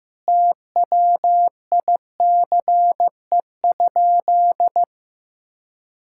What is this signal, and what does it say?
Morse code, heard as a keyed tone, about 15 wpm, 700 Hz. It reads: TWICE?
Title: Sentences from Top 1600 Words 15wpm